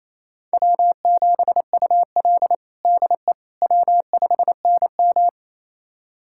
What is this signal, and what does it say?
Morse code, heard as a keyed tone, about 28 wpm, 700 Hz. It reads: W7UL DE W5NM
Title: USA Callsign DE Callsign - 4 Characters - 28wpm